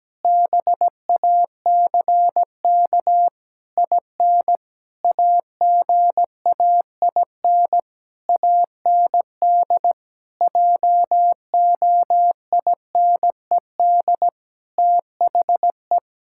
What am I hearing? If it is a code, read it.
Morse code, heard as a keyed tone, about 17 wpm, 700 Hz. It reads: BACK IN AGAIN AND JOINED THE
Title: Pangram Challenge v2 - 17wpm